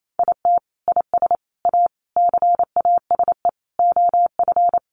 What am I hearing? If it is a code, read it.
Morse code, heard as a keyed tone, about 28 wpm, 700 Hz. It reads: IT IS A CASE OF
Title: Sentences from Top 1400 Words 28wpm